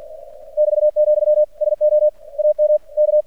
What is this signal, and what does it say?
Morse code, heard as a keyed tone, about 46 wpm, 600 Hz. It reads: <BT>, SO AM K
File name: archived / CRSnet